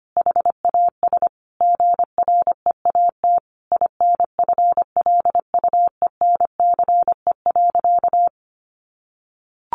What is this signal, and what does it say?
Morse code, heard as a keyed tone, about 25 wpm, 700 Hz. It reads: HAS GREAT INFLUENCE.
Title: Sentences from Top 1100 Words 25wpm